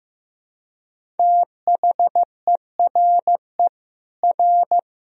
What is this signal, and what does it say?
Morse code, heard as a keyed tone, about 15 wpm, 700 Hz. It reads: THERE R
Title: Sentences from Top 200 Words and Abbreviations 15wpm